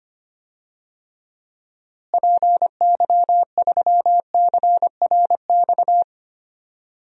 Morse code, heard as a keyed tone, about 25 wpm, 700 Hz. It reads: PY3CRX
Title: DXCC-100 Call Signs - 6 Characters - Repeated 3x - 25wpm